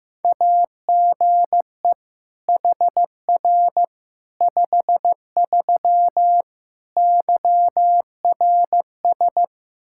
Morse code, heard as a keyed tone, about 16 wpm, 700 Hz. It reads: AGE HR 53 YRS
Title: Practice QSO - Exchange 3 - More Info - 15wpm